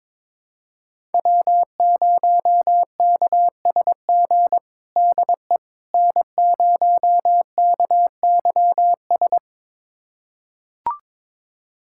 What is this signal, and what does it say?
Morse code, heard as a keyed tone, about 22 wpm, 700 Hz. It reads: W0KSG DE N0KYS
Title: USA Callsign DE Callsign - 5 Characters - 22wpm